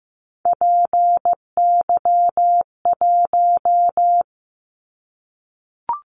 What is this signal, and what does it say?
Morse code, heard as a keyed tone, about 15 wpm, 700 Hz. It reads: PY1
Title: Callsign Prefixes for Top 100 DXCC - 15wpm